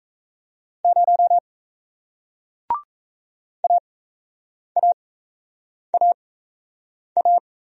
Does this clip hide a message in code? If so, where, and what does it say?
Yes: Morse code from 0.8 to 7.4 s, heard as a keyed tone, about 39 wpm, spaced as for 8 wpm, 700 Hz. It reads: 0 AAAA